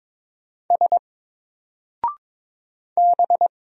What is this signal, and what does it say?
Morse code, heard as a keyed tone, about 22 wpm, 700 Hz. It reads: S B
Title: Single Character (Letter:Number:Punctuation:Prosign) 22wpm